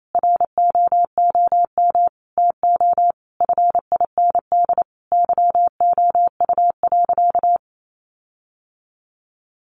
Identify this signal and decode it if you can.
Morse code, heard as a keyed tone, about 28 wpm, 700 Hz. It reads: ROOM TO FIND YOU.
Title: Sentences from Top 1300 Words 28wpm